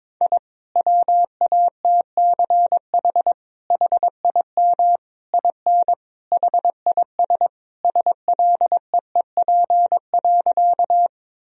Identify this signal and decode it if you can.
Morse code, heard as a keyed tone, about 22 wpm, 700 Hz. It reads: I WATCH HIM IN HIS SLEEP.
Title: Sentences from Top 1000 Words - Review Part 1 - 22wpm